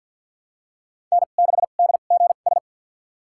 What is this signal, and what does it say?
Morse code, heard as a keyed tone, about 52 wpm, spaced as for 37 wpm, 700 Hz. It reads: N6DGS
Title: USA Call Signs - 5 Characters - 2x Character Spacing - 50wpm